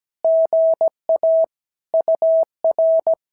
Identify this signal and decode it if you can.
Morse code, heard as a keyed tone, about 17 wpm, 650 Hz. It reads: GA UR
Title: SOTA QSO Practice 17wpm